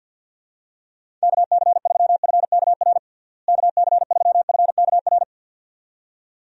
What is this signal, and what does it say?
Morse code, heard as a keyed tone, about 51 wpm, 700 Hz. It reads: KC3FKR KC3FKR
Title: USA Call Signs - 6 Characters - Repeated 2x - 50wpm